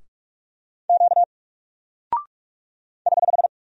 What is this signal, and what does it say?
Morse code, heard as a keyed tone, about 46 wpm, 700 Hz. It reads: Q <HH>
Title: Single Character (Letter:Number:Punctuation:Prosign) Alt 45wpm